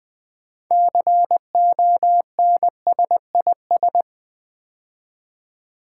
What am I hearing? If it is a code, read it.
Morse code, heard as a keyed tone, about 20 wpm, 700 Hz. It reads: CONSIS
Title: Build A Word - 6 Letters At A Time - 12-Letter Words - 20wpm